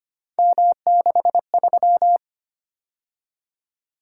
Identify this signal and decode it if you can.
Morse code, heard as a keyed tone, about 29 wpm, 700 Hz. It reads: M63
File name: Speed-Racing - Three Letter-Number Combinations - 25wpm